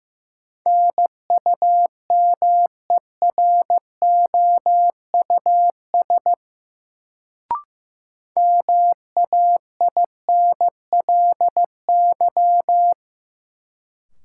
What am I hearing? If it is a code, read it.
Morse code, heard as a keyed tone, about 15 wpm, 700 Hz. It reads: NUMEROUS MAINLY
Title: Sentences from Top 1400 Words - New Word Drill - 15wpm